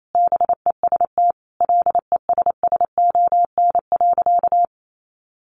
Morse code, heard as a keyed tone, about 28 wpm, 700 Hz. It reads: BEST LESSON.
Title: Sentences from Top 1400 Words 28wpm